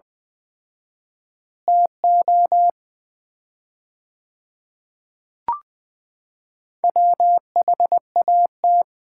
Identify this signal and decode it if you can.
Morse code, heard as a keyed tone, about 20 wpm, 700 Hz. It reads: TO WHAT
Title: 4x Word-Spacing - Sets of 5 Words - 20wpm